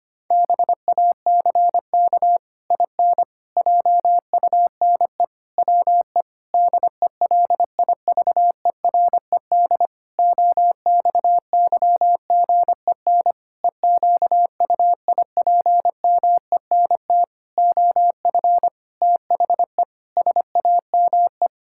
Morse code, heard as a keyed tone, about 25 wpm, 700 Hz. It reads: BACK IN JUNE WE DELIVERED OXYGEN EQUIPMENT OF THE SAME
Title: Pangram Challenge v2 - 25wpm